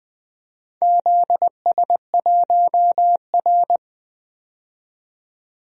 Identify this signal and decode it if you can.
Morse code, heard as a keyed tone, about 20 wpm, 700 Hz. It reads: ZS1R
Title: DXCC-100 Call Signs - 4 Characters - 20wpm